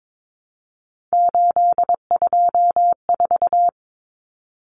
Morse code, heard as a keyed tone, about 22 wpm, 700 Hz. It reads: 824
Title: Common Cut Numbers - 3-Digit 22wpm